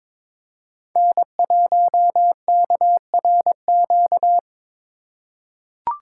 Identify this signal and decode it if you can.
Morse code, heard as a keyed tone, about 22 wpm, 700 Hz. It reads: N1KRQ
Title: USA Call Signs - 5 Characters - 2x Character Spacing - 22wpm